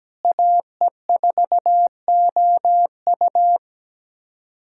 Morse code, heard as a keyed tone, about 17 wpm, 700 Hz. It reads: AE4OU
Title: USA Callsign DE Callsign - 5 Characters - 17wpm